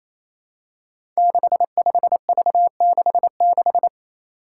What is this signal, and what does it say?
Morse code, heard as a keyed tone, about 27 wpm, 700 Hz. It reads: 65V66
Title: Five Challenging Letter-Number Combinations 28wpm